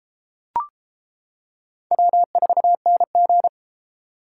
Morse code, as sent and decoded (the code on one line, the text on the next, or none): .-- ....- -. --.
W4NG